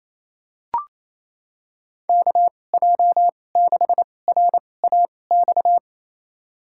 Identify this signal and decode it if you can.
Morse code, heard as a keyed tone, about 27 wpm, spaced as for 20 wpm, 700 Hz. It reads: KJ6RAX
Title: USA Call Signs - 2x Character Spacing - 28wpm